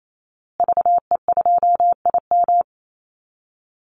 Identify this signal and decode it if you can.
Morse code, heard as a keyed tone, about 28 wpm, 700 Hz. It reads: VE2IM
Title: Call Signs Repeated 2x - 28wpm